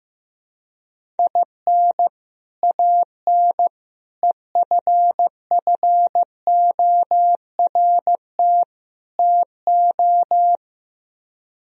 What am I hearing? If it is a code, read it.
Morse code, heard as a keyed tone, about 15 wpm, 700 Hz. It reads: IN AN EFFORT TO
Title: Speed-Racing - Sets of 4 Words 15wpm